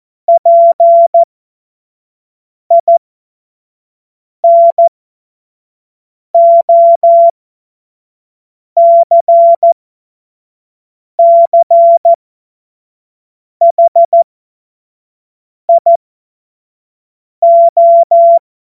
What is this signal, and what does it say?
Morse code, heard as a keyed tone, about 14 wpm, spaced as for 5 wpm, 680 Hz. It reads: PINOCCHIO
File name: Morsekurs / Witze / Witz 09. November 2021, 25 F70